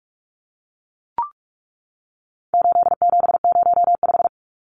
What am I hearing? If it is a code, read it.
Morse code, heard as a keyed tone, about 44 wpm, 700 Hz. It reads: 8705
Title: Numbers - 4-Digit 45wpm